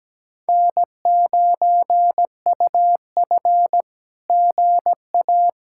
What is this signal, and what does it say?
Morse code, heard as a keyed tone, about 17 wpm, 700 Hz. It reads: N9UF GA
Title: SOTA QSO Practice 17wpm